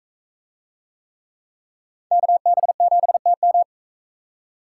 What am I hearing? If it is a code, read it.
Morse code, heard as a keyed tone, about 37 wpm, 700 Hz. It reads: KB7TM